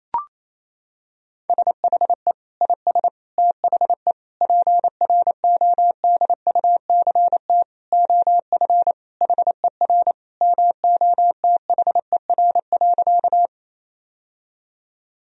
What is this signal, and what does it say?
Morse code, heard as a keyed tone, about 27 wpm, 700 Hz. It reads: SHE IS THE PRODUCT OF HER MOTHER.
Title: Sentences from Top 1000 Words - Review Part 2 - 28wpm